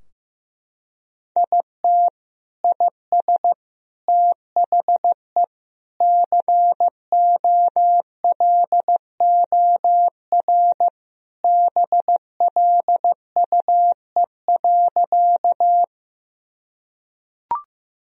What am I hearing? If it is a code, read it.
Morse code, heard as a keyed tone, about 15 wpm, 700 Hz. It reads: IT IS THE COLOR BLUE.
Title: Sentences from Top 1000 Words - Review Part 3 - 15wpm